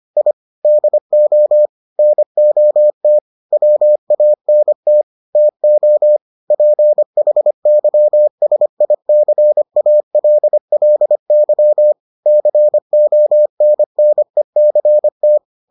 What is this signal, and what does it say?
Morse code, heard as a keyed tone, about 25 wpm, 600 Hz. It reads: I DO NOT WANT TO PHYSICALLY CONNECT